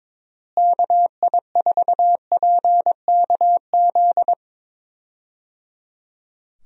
Morse code, heard as a keyed tone, about 22 wpm, 700 Hz. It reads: KI4PKZ
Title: USA Call Signs - 6 Characters - Repeated 3x - 22wpm